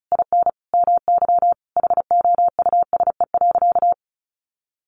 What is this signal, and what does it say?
Morse code, heard as a keyed tone, about 34 wpm, 700 Hz. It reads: IN MY HOUSE.
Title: Sentences from Top 1100 Words 35wpm